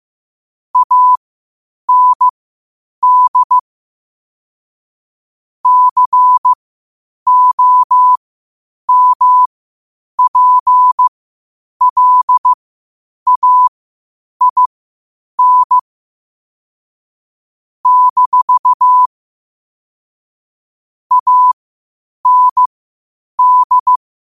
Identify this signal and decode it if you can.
Morse code, heard as a keyed tone, about 15 wpm, spaced as for 8 wpm, 1000 Hz. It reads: AND COMPLAIN - AND